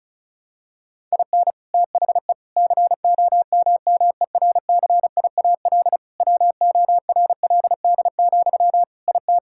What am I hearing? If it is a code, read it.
Morse code, heard as a keyed tone, about 36 wpm, 700 Hz. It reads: IN THE COMMERCIAL WORLD, IT
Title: Sentences from Top 1000 Words - Review Part 5 - 35wpm